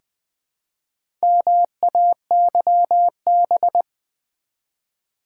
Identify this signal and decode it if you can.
Morse code, heard as a keyed tone, about 20 wpm, 700 Hz. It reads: MAYB
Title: Build A Word - 1 Letter At A Time - 5-Letter Words - 20wpm